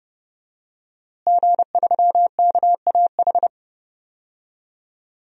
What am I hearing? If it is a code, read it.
Morse code, heard as a keyed tone, about 31 wpm, 700 Hz. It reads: G3KAH
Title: DXCC-100 Call Signs - 5 Characters - Repeated 2x - 30wpm